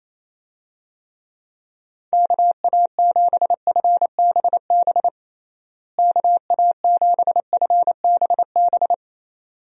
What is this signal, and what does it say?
Morse code, heard as a keyed tone, about 28 wpm, 700 Hz. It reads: KA7FBB KA7FBB
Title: USA Call Signs - 6 Characters - Repeated 2x - 28wpm